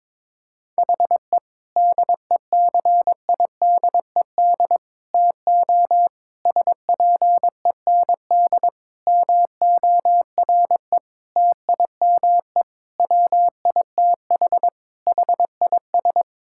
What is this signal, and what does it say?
Morse code, heard as a keyed tone, about 21 wpm, 700 Hz. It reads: HE DECIDED TO SPEND MORE TIME WITH HIS
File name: Sentences with 7-Word Phrases - 22wpm